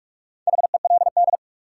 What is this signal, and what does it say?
Morse code, heard as a keyed tone, about 46 wpm, 700 Hz. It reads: HELD
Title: Sentences from Top 1600 Words - New Word Drill Alt - 45wpm